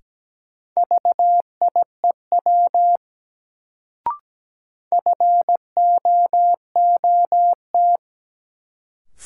Morse code, heard as a keyed tone, about 17 wpm, 700 Hz. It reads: VIEW FOOT
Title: Sets of 5 Words - New Word Drill - 17wpm